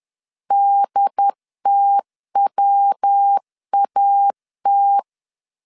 Morse code, heard as a keyed tone, about 10 wpm, 800 Hz. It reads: DTWAT
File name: CW / DET / 1 3 1500C S B 10wpm 3335